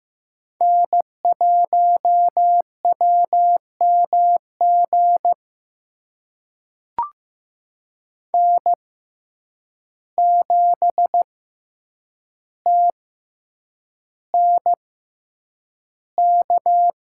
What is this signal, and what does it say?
Morse code, heard as a keyed tone, about 15 wpm, spaced as for 8 wpm, 700 Hz. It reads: N1WMG N7TNK
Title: USA Call Signs - 5 Characters - 6x Character Spacing - 15wpm